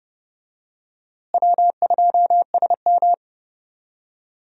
Morse code, heard as a keyed tone, about 29 wpm, 700 Hz. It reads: W2SM
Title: USA Call Signs - 4 Characters - Repeated 3x - 30wpm